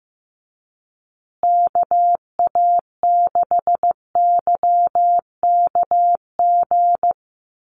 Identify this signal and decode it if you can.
Morse code, heard as a keyed tone, about 14 wpm, 700 Hz. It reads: KA6YKG
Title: USA Call Signs Repeated 2x 15wpm